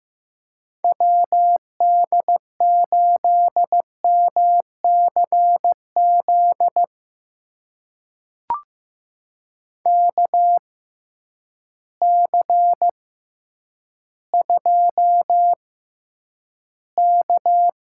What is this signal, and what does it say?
Morse code, heard as a keyed tone, about 15 wpm, spaced as for 9 wpm, 700 Hz. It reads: WD8MCZ KC2K
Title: USA Call Signs - 6 Characters - 6x Character Spacing - 15wpm